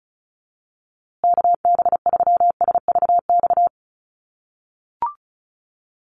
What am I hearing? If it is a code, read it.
Morse code, heard as a keyed tone, about 34 wpm, 700 Hz. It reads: KB3SVX
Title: USA Call Signs Repeated 2x 35wpm